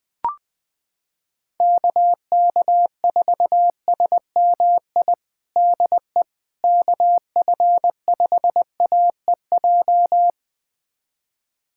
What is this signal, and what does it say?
Morse code, heard as a keyed tone, about 20 wpm, 700 Hz. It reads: KK4SMI DE KF5AEJ
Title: USA Callsign DE Callsign - 6 Characters - 20wpm